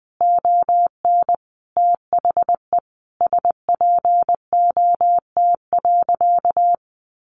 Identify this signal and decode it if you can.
Morse code, heard as a keyed tone, about 20 wpm, 700 Hz. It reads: ON THE SPOT.
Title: Sentences from Top 1100 Words 20wpm